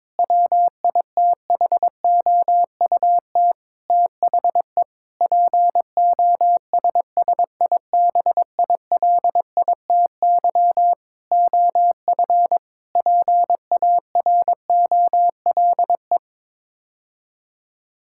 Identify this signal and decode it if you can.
Morse code, heard as a keyed tone, about 21 wpm, 700 Hz. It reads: WITHOUT THE POSSIBILITY OF PAROLE
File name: practice / 22wpm / Sets of 8 Words - 22wpm